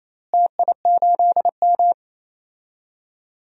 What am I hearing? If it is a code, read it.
Morse code, heard as a keyed tone, about 28 wpm, 700 Hz. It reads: TI8M
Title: DXCC-100 Call Signs - 4 Characters - Repeated 2x - 28wpm